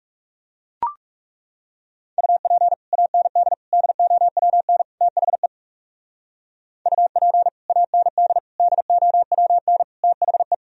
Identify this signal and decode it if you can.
Morse code, heard as a keyed tone, about 37 wpm, 700 Hz. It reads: UP AND DOWN THE UP AND DOWN THE
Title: Speed-Racing - Sets of 4 Words 30wpm